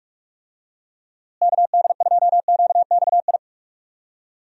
Morse code, heard as a keyed tone, about 41 wpm, 700 Hz. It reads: KD2QXI
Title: USA Call Signs - 6 Characters - Repeated 3x - 45wpm